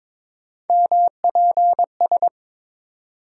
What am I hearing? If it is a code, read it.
Morse code, heard as a keyed tone, about 22 wpm, 700 Hz. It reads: MPS
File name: practice / 22wpm / Three Letter-Number Combinations - 22wpm